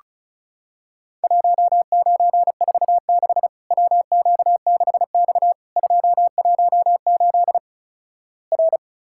Morse code, heard as a keyed tone, about 36 wpm, 700 Hz. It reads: 1946 WQ6X 218 R
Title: Worked All Europe DX Contest - QTC Exchange - 35wpm